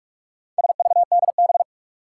Morse code, heard as a keyed tone, about 46 wpm, 700 Hz. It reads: SVDB
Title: Four Challenging Letter-Number Combinations 45wpm